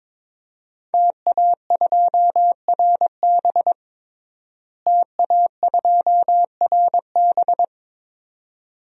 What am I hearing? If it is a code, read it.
Morse code, heard as a keyed tone, about 22 wpm, 700 Hz. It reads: TA2RB TA2RB
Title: DXCC-100 Call Signs - 5 Characters - Repeated 2x - 22wpm